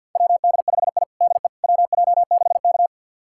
Morse code, heard as a keyed tone, about 45 wpm, 700 Hz. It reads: WD5I DE WJ6K
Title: USA Callsign DE Callsign - 4 Characters - 50wpm